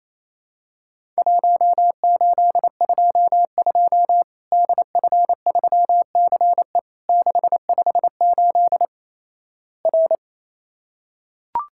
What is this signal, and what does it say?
Morse code, heard as a keyed tone, about 28 wpm, 700 Hz. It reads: 1822 DF3CE 658 R
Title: Worked All Europe DX Contest - QTC Exchange - 28wpm